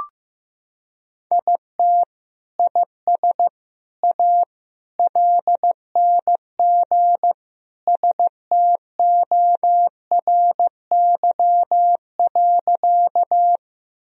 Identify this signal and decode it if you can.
Morse code, heard as a keyed tone, about 15 wpm, 700 Hz. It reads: IT IS A LNG STORY.
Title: Sentences from Top 300 Words and Abbreviations 15wpm